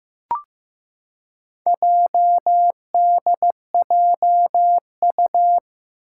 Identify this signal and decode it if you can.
Morse code, heard as a keyed tone, about 15 wpm, 700 Hz. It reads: JDJU
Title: Four Challenging Letter-Number Combinations 15wpm